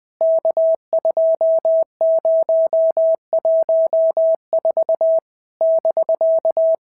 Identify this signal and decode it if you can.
Morse code, heard as a keyed tone, about 19 wpm, 650 Hz. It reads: K2014 <BK>
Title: POTA QSO Practice - 20wpm